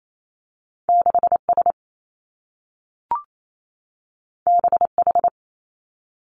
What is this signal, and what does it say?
Morse code, heard as a keyed tone, about 28 wpm, 700 Hz. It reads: 6 S B H
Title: Two Challenging Letter-Number Combinations 28wpm